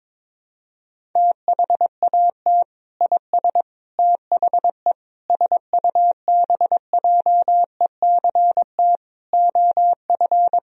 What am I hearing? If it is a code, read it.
Morse code, heard as a keyed tone, about 22 wpm, 700 Hz. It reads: THAT IS THE SUBJECT OF
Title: Sentences from Top 1000 Words - Review Part 4 - 22wpm